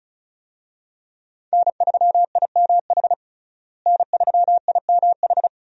Morse code, heard as a keyed tone, about 36 wpm, 700 Hz. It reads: N3IMH N3IMH
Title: USA Call Signs - 5 Characters - Repeated 2x - 35wpm